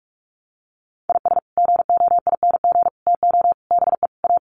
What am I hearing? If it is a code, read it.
Morse code, heard as a keyed tone, about 44 wpm, 700 Hz. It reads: IS GOING TO BE A
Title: Sets of 5 Words 45wpm